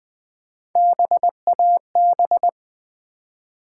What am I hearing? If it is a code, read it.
Morse code, heard as a keyed tone, about 20 wpm, 700 Hz. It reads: BAB
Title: Build A Word - 1 Letter At A Time - 3 to 5-Letter Words - 20wpm